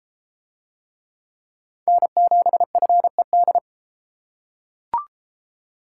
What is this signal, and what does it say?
Morse code, heard as a keyed tone, about 30 wpm, 700 Hz. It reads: N7FED